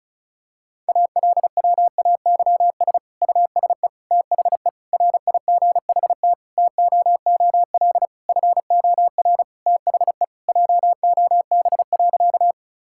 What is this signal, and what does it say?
Morse code, heard as a keyed tone, about 36 wpm, 700 Hz. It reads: ALWAYS USE THE RIGHT TOOL FOR THE JOB.
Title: Sentences from Top 1000 Words - Review Part 6 - 35wpm